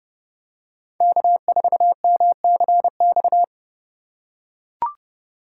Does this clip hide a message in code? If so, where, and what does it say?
Yes: Morse code from 1.0 to 3.4 s, heard as a keyed tone, about 29 wpm, 700 Hz. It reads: K4MCX